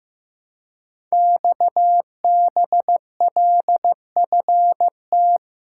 Morse code, heard as a keyed tone, about 15 wpm, 700 Hz. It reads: XBLFT
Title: Code Groups - 5 Letters - 15wpm